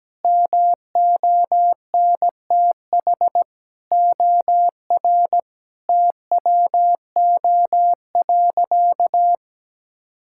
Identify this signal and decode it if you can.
Morse code, heard as a keyed tone, about 17 wpm, 700 Hz. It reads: MONTH OR TWO.
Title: Sentences from Top 1000 Words - Review Part 5 - 17wpm